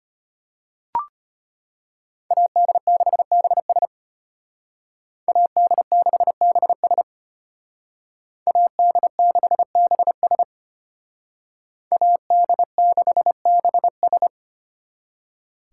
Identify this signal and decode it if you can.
Morse code, heard as a keyed tone, about 31 wpm, 700 Hz. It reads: AD6BS AD6BS AD6BS AD6BS
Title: Speed Racing - USA Call Signs 25wpm